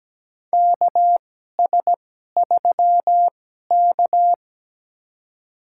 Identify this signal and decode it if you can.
Morse code, heard as a keyed tone, about 17 wpm, spaced as for 12 wpm, 700 Hz. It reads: KS3K
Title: USA Call Signs - 4 Characters - 2x Character Spacing - 17wpm